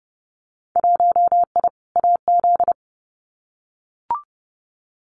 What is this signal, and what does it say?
Morse code, heard as a keyed tone, about 30 wpm, 700 Hz. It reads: 1I AZ
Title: Winter Field Day Exchanges 30wpm